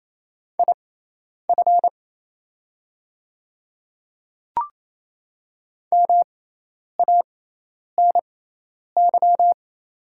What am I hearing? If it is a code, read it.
Morse code, heard as a keyed tone, about 28 wpm, spaced as for 10 wpm, 700 Hz. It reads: IF MANY
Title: 6x Character Spacing - Top 200 Words - 28wpm